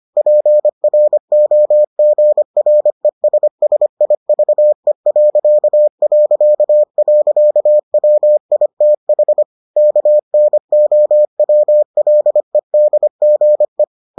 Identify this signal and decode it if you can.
Morse code, heard as a keyed tone, about 25 wpm, 600 Hz. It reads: PROGRESSIVE...WITH KNOWLEDGE